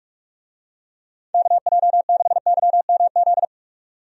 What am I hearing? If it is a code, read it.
Morse code, heard as a keyed tone, about 46 wpm, 700 Hz. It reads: KJ6YMZ
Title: USA Call Signs - 6 Characters - 45wpm